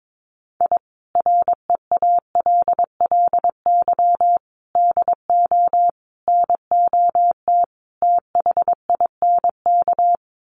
Morse code, heard as a keyed tone, about 22 wpm, 700 Hz. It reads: I REALLY DO NOT THINK
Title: Sets of 5 Words Encore2 22wpm